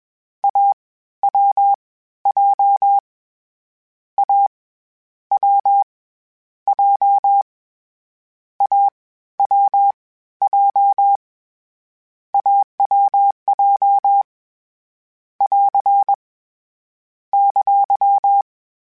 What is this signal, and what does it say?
Morse code, heard as a keyed tone, about 21 wpm, 800 Hz. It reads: A W J A W J A W J AWJ <AR> !
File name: Difficult Combos / Difficult Combos-mp3s / AWJ @20